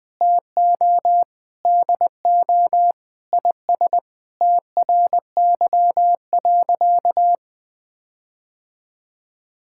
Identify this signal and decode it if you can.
Morse code, heard as a keyed tone, about 20 wpm, 700 Hz. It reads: TO DO IS TRY.
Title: Sentences with 5-Word Phrases - 20wpm